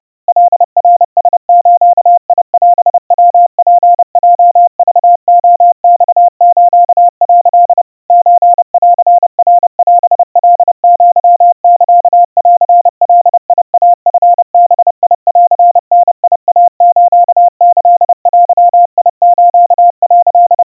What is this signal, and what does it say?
Morse code (Japanese Wabun code), heard as a keyed tone, about 30 wpm, 700 Hz. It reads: カナラズオヤツヲクレマス」 ソンナオカアサンガイチバンダイスキデス」
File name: kawa / Morse sample / kamyu 110 cpm 700 Hz Crispy